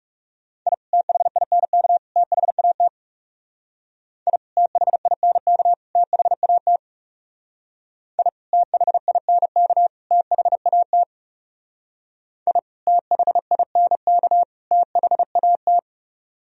Morse code, heard as a keyed tone, about 36 wpm, 700 Hz. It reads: I THINK THAT I THINK THAT I THINK THAT I THINK THAT
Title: Speed-Racing - Sets of 3 Words 30wpm